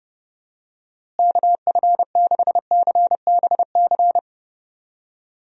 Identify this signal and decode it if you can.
Morse code, heard as a keyed tone, about 30 wpm, 700 Hz. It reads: KF6CBC
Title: USA Call Signs - 6 Characters - Repeated 2x - 30wpm